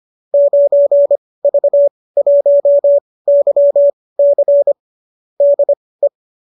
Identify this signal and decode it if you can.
Morse code, heard as a keyed tone, about 25 wpm, spaced as for 18 wpm, 590 Hz. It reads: 9V1YC DE
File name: qso207 18